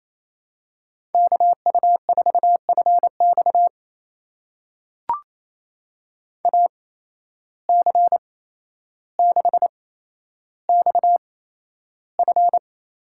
Morse code, heard as a keyed tone, about 28 wpm, spaced as for 12 wpm, 700 Hz. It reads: KU4FX AC6XF
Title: USA Call Signs - 5 Characters - 8x Character Spacing - 28wpm